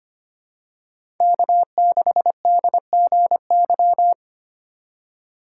Morse code, heard as a keyed tone, about 25 wpm, 700 Hz. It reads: K6DGY
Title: USA Call Signs - 5 Characters - 2x Character Spacing - 25wpm